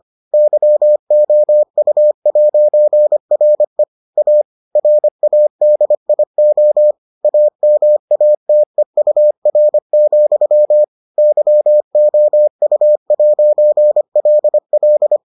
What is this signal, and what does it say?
Morse code, heard as a keyed tone, about 25 wpm, 600 Hz. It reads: YOU'RE A RADIO AMATEUR, YOU'LL